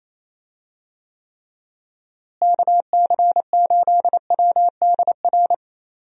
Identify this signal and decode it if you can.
Morse code, heard as a keyed tone, about 28 wpm, 700 Hz. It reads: KC8WDR
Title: USA Call Signs - 6 Characters - Repeated 2x - 28wpm